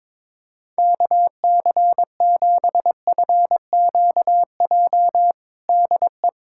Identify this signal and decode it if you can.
Morse code, heard as a keyed tone, about 22 wpm, 700 Hz. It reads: KC7FQJ DE
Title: USA Callsign DE Callsign - 6 Characters - 22wpm